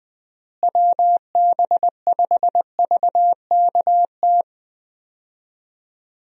Morse code, heard as a keyed tone, about 21 wpm, 700 Hz. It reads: WB5VKT
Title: USA Call Signs - 6 Characters - Repeated 3x - 20wpm